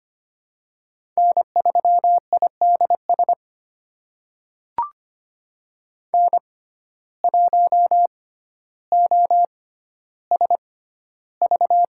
Morse code, heard as a keyed tone, about 25 wpm, spaced as for 13 wpm, 700 Hz. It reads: N3IDS N1OSV
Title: USA Call Signs - 5 Characters - 6x Character Spacing - 25wpm